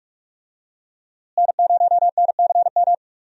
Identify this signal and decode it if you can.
Morse code, heard as a keyed tone, about 46 wpm, 700 Hz. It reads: N0NCM
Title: USA Call Signs - 5 Characters - Repeated 3x - 45wpm